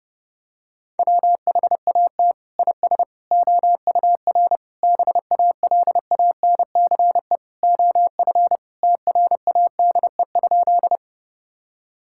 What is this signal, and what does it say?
Morse code, heard as a keyed tone, about 32 wpm, 700 Hz. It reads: WHAT IS OUR BALANCE OF TRADE?
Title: Sentences from Top 1000 Words - Review Part 6 - 30wpm